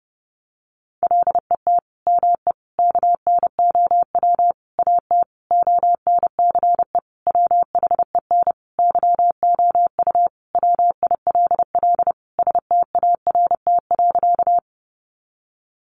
Morse code, heard as a keyed tone, about 30 wpm, 700 Hz. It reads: LET ME KNOW AT ONCE WHEN YOU WILL START.
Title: Sentences from Top 300 Words 30wpm